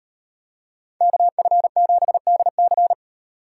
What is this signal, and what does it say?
Morse code, heard as a keyed tone, about 38 wpm, 700 Hz. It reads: KF7DC